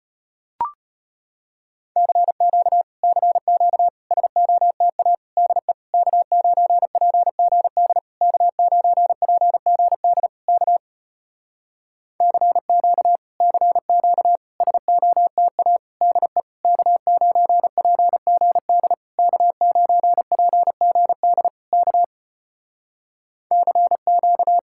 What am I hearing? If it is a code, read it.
Morse code, heard as a keyed tone, about 33 wpm, 700 Hz. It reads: CQ CQ SOTA DE K9PGD K9PGD K CQ CQ SOTA DE K9PGD K9PGD K CQ